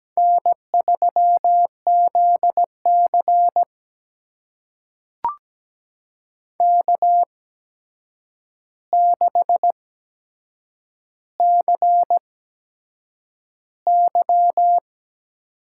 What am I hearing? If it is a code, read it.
Morse code, heard as a keyed tone, about 17 wpm, spaced as for 7 wpm, 700 Hz. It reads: N3ZC K6CY
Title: USA Call Signs - 4 Characters - 8x Character Spacing - 17wpm